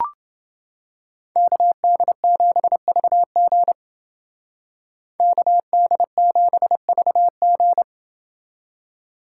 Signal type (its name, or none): Morse code